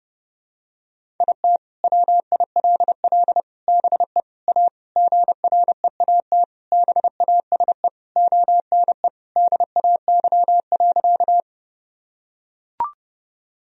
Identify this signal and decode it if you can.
Morse code, heard as a keyed tone, about 32 wpm, 700 Hz. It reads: IT WILL BE A GREAT BASE ONE DAY.
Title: Sentences from Top 1000 Words - Review Part 1 - 30wpm